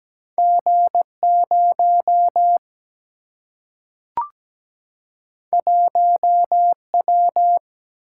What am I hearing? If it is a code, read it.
Morse code, heard as a keyed tone, about 17 wpm, 700 Hz. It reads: G0 1W
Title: Single-Letter Single-Number 17wpm